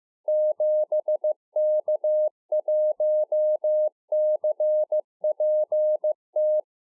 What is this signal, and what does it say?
Morse code, heard as a keyed tone, about 15 wpm, 610 Hz. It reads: 7K1CPT